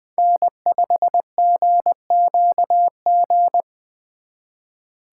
Morse code, heard as a keyed tone, about 24 wpm, 700 Hz. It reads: N5GQG
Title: Speed Racing - USA 1x3 Call Signs 20wpm